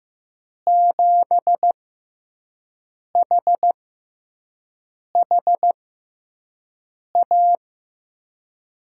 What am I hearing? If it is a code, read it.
Morse code, heard as a keyed tone, about 15 wpm, spaced as for 5 wpm, 700 Hz. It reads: 7HHA
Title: USA Call Signs - 6 Characters - 6x Character Spacing - 15wpm